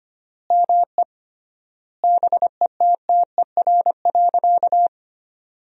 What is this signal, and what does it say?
Morse code, heard as a keyed tone, about 25 wpm, 700 Hz. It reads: ME BETTER.
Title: 3x Word-Spacing - Sentences from Top 400 Words - 25wpm